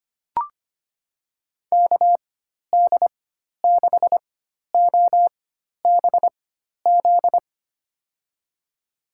Morse code, heard as a keyed tone, about 25 wpm, spaced as for 12 wpm, 700 Hz. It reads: KD6OBZ
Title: USA Call Signs - 6 Characters - 4x Character Spacing - 25wpm